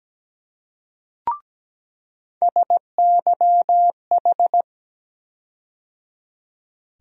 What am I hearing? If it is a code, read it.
Morse code, heard as a keyed tone, about 17 wpm, 700 Hz. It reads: SYH
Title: USA Call Sign Postfixes - 3 Letters 17wpm